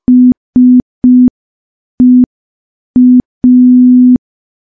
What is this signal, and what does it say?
Morse code, heard as a keyed tone, about 5 wpm, 250 Hz. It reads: SEA
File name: morse / morse-training-05WPM